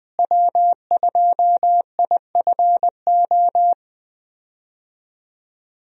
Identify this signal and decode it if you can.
Morse code, heard as a keyed tone, about 20 wpm, 700 Hz. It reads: W2IFO
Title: USA Call Signs - 5 Characters - Repeated 2x - 20wpm